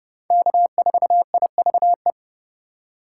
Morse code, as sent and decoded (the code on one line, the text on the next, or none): -.- ....- .. ...- .
K4IVE